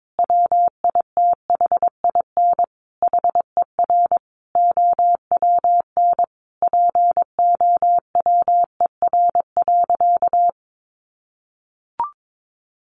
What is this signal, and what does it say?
Morse code, heard as a keyed tone, about 22 wpm, 700 Hz. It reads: WITHIN HER OWN POWER.
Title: Sentences from Top 1200 Words 22wpm